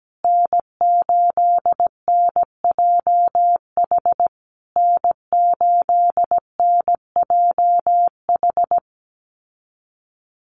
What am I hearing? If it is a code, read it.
Morse code, heard as a keyed tone, about 17 wpm, 700 Hz. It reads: N8NJH N8NJH
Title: USA Call Signs Repeated 3x 17wpm